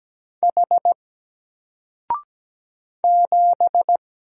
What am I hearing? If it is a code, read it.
Morse code, heard as a keyed tone, about 17 wpm, 700 Hz. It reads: H 7
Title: Single Letter-Number 17wpm